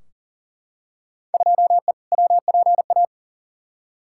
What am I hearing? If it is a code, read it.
Morse code, heard as a keyed tone, about 40 wpm, 700 Hz. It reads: 2E WPA
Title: Field Day Exchanges Encore - 40wpm